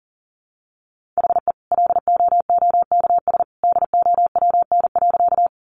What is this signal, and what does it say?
Morse code, heard as a keyed tone, about 40 wpm, 700 Hz. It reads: HE LOOKS DOWN.
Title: Additional Easy Phrases 40wpm